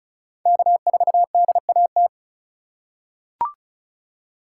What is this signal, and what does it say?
Morse code, heard as a keyed tone, about 36 wpm, 700 Hz. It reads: K4DAT
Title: USA Call Signs - 5 Characters - 3x Character Spacing - 35wpm